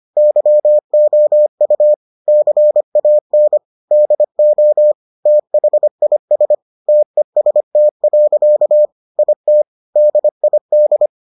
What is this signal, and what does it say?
Morse code, heard as a keyed tone, about 25 wpm, 600 Hz. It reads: YOU CAN DO THIS TEST. IT DID